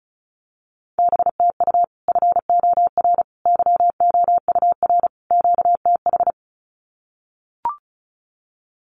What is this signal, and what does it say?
Morse code, heard as a keyed tone, about 34 wpm, 700 Hz. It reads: BTU FOR YOUR QTH
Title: Common QSO Phrases 35wpm